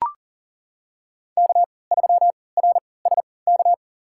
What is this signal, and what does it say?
Morse code, heard as a keyed tone, about 40 wpm, spaced as for 23 wpm, 700 Hz. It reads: K3RSK
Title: USA Call Signs - 5 Characters - 3x Character Spacing - 40wpm